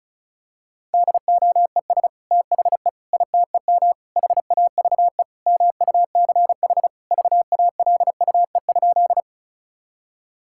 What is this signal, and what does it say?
Morse code, heard as a keyed tone, about 34 wpm, 700 Hz. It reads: DOES THE ITEM HAVE MUCH VALUE?
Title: Sentences from Top 1000 Words - Review Part 6 - 35wpm